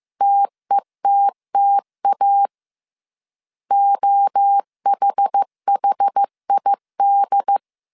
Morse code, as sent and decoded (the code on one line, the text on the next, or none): - . - - .- / --- .... .... .. -..
TETTA OHHID